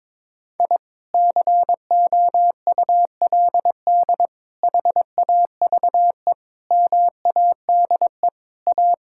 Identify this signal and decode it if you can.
Morse code, heard as a keyed tone, about 22 wpm, 700 Hz. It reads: I COULD HAVE MADE A
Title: Sentences from Top 1000 Words - Review Part 5 - 22wpm